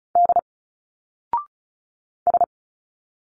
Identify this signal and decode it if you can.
Morse code, heard as a keyed tone, about 33 wpm, 700 Hz. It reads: D S
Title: Single Letters 35wpm